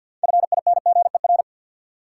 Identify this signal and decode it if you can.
Morse code, heard as a keyed tone, about 52 wpm, 700 Hz. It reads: FINGER
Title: Six Letter Words - 50wpm